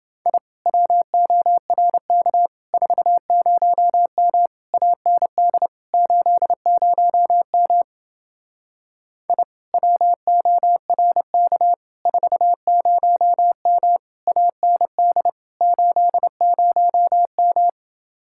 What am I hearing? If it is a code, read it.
Morse code, heard as a keyed tone, about 29 wpm, 700 Hz. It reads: I WORK 40M AND 80M I WORK 40M AND 80M